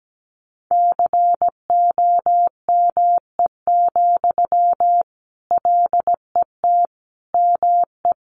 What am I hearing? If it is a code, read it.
Morse code, heard as a keyed tone, about 17 wpm, 700 Hz. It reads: COME, LET ME
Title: Sentences from Top 500 Words 17wpm